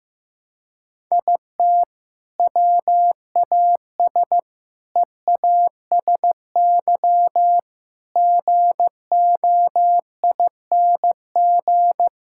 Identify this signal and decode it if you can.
Morse code, heard as a keyed tone, about 15 wpm, 700 Hz. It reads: IT WAS EASY GOING
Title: Sentences from Top 1000 Words - Review Part 5 - 15wpm